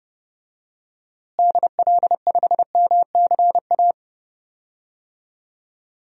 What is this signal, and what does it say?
Morse code, heard as a keyed tone, about 30 wpm, 700 Hz. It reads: DL5MCA
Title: DXCC-100 Call Signs - 6 Characters - Repeated 2x - 30wpm